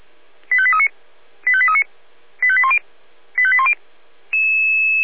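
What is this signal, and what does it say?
Five-tone selective call (ZVEI), heard as quick runs of five tones, tones of about 70 ms. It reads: 85839 85839 85810 85810
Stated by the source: ZVEI five-tone selective call received with an SDR.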